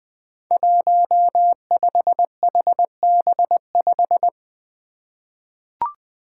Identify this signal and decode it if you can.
Morse code, heard as a keyed tone, about 20 wpm, 700 Hz. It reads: 15HB5
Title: Five Challenging Letter-Number Combinations 20wpm